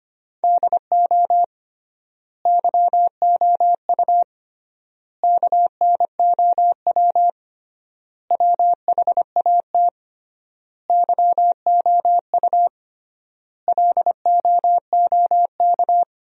Morse code, heard as a keyed tone, about 25 wpm, 700 Hz. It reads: DO YOU KNOW WHAT YOU LOOK
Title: 3x Word-Spacing - Sentences from Top 100 Words - 25wpm